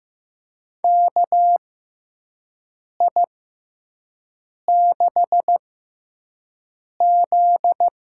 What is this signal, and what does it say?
Morse code, heard as a keyed tone, about 15 wpm, spaced as for 5 wpm, 700 Hz. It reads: KI6Z
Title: USA Call Signs - 6 Characters - 6x Character Spacing - 15wpm